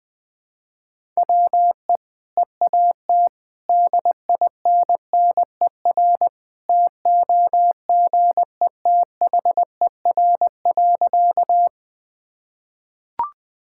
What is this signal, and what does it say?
Morse code, heard as a keyed tone, about 19 wpm, 700 Hz. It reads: WE EAT DINNER TOGETHER.
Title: Easy Sentences with Hints - 20wpm